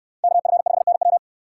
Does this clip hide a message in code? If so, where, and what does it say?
Yes: Morse code from 0.2 to 1.2 s, heard as a keyed tone, about 69 wpm, 700 Hz. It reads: KF5NW